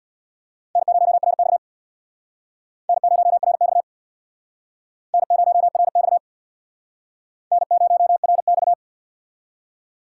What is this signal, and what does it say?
Morse code, heard as a keyed tone, about 62 wpm, 700 Hz. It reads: N0RX N0RX N0RX N0RX
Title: Speed Racing - USA 1x2 Call Signs 50wpm